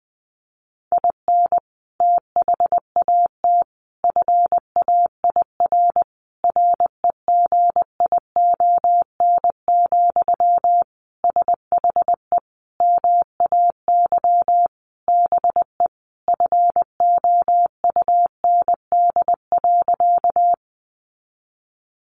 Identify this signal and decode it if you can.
Morse code, heard as a keyed tone, about 20 wpm, 700 Hz. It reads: IN THAT FAIR REGION, SHE MAY BE FOUND.
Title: Sentences from Top 1500 Words 20wpm